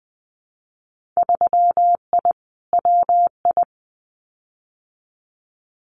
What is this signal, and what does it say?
Morse code, heard as a keyed tone, about 19 wpm, 700 Hz. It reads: 3I WI
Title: Winter Field Day Exchanges 20wpm